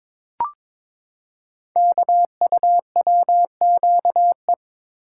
Morse code, heard as a keyed tone, about 22 wpm, 700 Hz. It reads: KUWQE
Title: Code Groups - 5 Letters - 22wpm